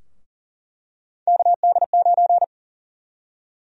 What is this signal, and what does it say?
Morse code, heard as a keyed tone, about 37 wpm, 700 Hz. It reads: KD9
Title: USA Call Sign Prefixes - 40wpm